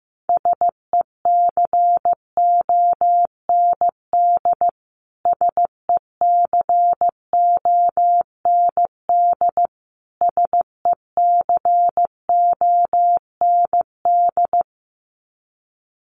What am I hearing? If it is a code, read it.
Morse code, heard as a keyed tone, about 15 wpm, 700 Hz. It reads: SECOND SECOND SECOND
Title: Top 300 Words Repeated 15wpm